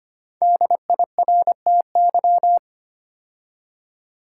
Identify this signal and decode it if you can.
Morse code, heard as a keyed tone, about 25 wpm, 700 Hz. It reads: DIRTY
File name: 5 Letter Words - First 3 Letters Given - 25wpm